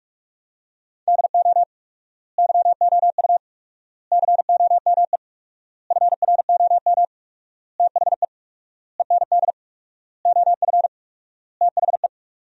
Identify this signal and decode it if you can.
Morse code, heard as a keyed tone, about 46 wpm, 700 Hz. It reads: DO YOU COME FROM THE END OF THE
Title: 4x Word-Spacing - Sentences from Top 200 Words - 45wpm